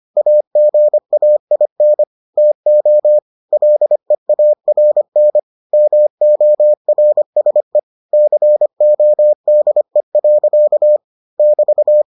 Morse code, heard as a keyed tone, about 25 wpm, 600 Hz. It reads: AGAIN TO LEARN MORSE CODE. <BT>